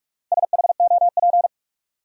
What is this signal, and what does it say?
Morse code, heard as a keyed tone, about 44 wpm, 700 Hz. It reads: SHOP
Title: Sentences from Top 1600 Words - New Word Drill Alt - 45wpm